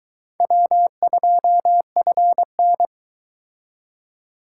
Morse code, heard as a keyed tone, about 20 wpm, 700 Hz. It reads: W2FN